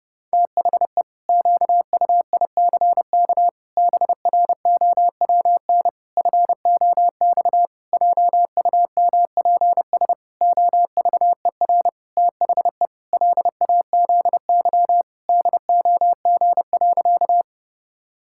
Morse code, heard as a keyed tone, about 30 wpm, 700 Hz. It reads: THE QUICK BROWN FOX JUMPS OVER THE LAZY DOG.
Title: Pangram Challenge v2 - 30wpm